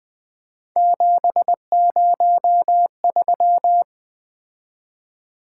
Morse code, heard as a keyed tone, about 25 wpm, 700 Hz. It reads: 703
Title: Speed-Racing - Three Letter-Number Combinations - 20wpm